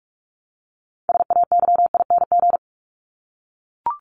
Morse code, heard as a keyed tone, about 44 wpm, 700 Hz. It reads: SAYING
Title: ING Suffix - Top 100 Words with Base Words 45wpm